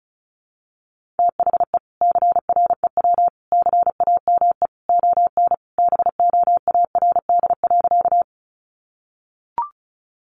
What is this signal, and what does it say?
Morse code, heard as a keyed tone, about 34 wpm, 700 Hz. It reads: THE CREW CAME ON BOARD.
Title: Sentences from Top 1500 Words 35wpm